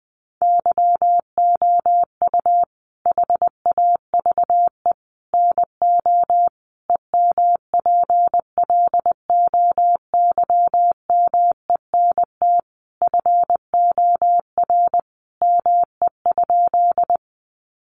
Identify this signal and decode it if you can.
Morse code, heard as a keyed tone, about 20 wpm, 700 Hz. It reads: YOU HAVE NO EMPLOYMENT FOR ME?
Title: Sentences from Top 1500 Words 20wpm